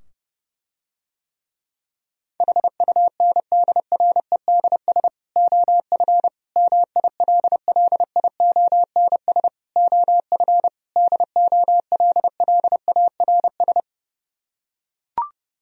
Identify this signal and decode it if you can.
Morse code, heard as a keyed tone, about 30 wpm, 700 Hz. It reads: HUNDREDS OF MILLIONS OF DOLLARS